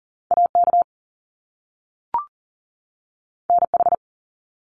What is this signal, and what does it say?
Morse code, heard as a keyed tone, about 39 wpm, 700 Hz. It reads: AK NH
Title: US States 40wpm